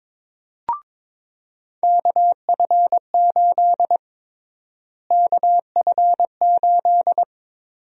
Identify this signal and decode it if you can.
Morse code, heard as a keyed tone, about 22 wpm, 700 Hz. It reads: KF8 KF8
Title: USA Call Sign Prefixes - Repeated 2x - 22wpm